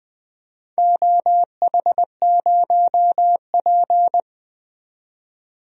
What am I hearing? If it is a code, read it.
Morse code, heard as a keyed tone, about 20 wpm, 700 Hz. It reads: OH0P
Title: DXCC-100 Call Signs - 4 Characters - 20wpm